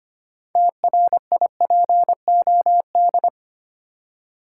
Morse code, heard as a keyed tone, about 25 wpm, 690 Hz. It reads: TRIPOD
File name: practice / 25wpm / Easy Words with Hints - 25wpm